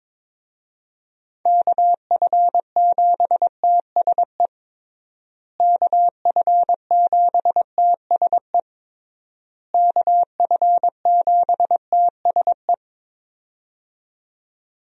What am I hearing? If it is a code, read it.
Morse code, heard as a keyed tone, about 21 wpm, 700 Hz. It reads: KF7TSE KF7TSE KF7TSE
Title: USA Call Signs - 6 Characters - Repeated 3x - 22wpm